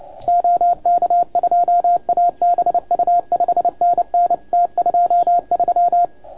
Morse code, heard as a keyed tone, about 30 wpm, 690 Hz. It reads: OK2ABU 5NN T23
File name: Archiv / CUC287 / OK1FGD CUC 287